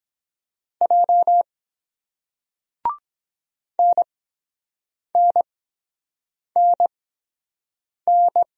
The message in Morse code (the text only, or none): J NNNN